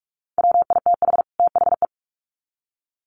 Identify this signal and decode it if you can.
Morse code, heard as a keyed tone, about 44 wpm, 700 Hz. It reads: WITH THE
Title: Sets of 2 Words 45wpm